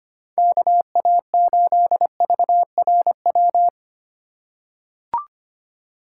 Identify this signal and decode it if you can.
Morse code, heard as a keyed tone, about 25 wpm, 700 Hz. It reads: KA8VRW
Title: USA Call Signs - 6 Characters - 3x Character Spacing - 25wpm